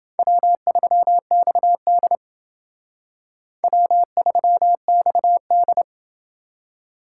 Morse code, heard as a keyed tone, about 28 wpm, 700 Hz. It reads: W3XD W3XD